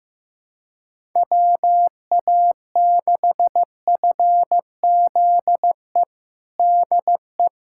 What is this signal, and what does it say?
Morse code, heard as a keyed tone, about 15 wpm, 700 Hz. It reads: WA6FZE DE
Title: USA Callsign DE Callsign - 6 Characters - 15wpm